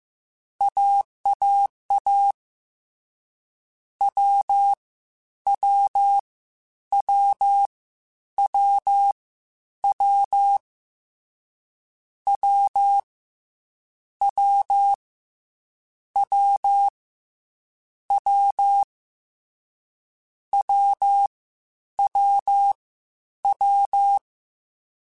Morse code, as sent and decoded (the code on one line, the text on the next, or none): .- .- .- / .-- / .-- / .-- / .-- / .-- / .-- / .-- / .-- / .-- / .-- / .-- / .--
AAA W W W W W W W W W W W W